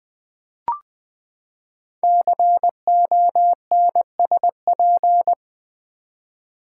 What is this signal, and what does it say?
Morse code, heard as a keyed tone, about 20 wpm, 700 Hz. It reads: CONSP
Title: Build A Word - 5 Letters At A Time - 10-Letter Words - 20wpm